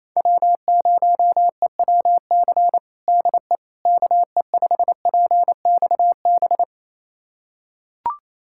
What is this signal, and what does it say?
Morse code, heard as a keyed tone, about 28 wpm, 700 Hz. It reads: W0EWC DE KE5PXB
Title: USA Callsign DE Callsign 28wpm